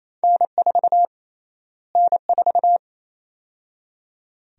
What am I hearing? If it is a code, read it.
Morse code, heard as a keyed tone, about 28 wpm, 700 Hz. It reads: N4 N4
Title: USA Call Sign Prefixes - Repeated 2x - 28wpm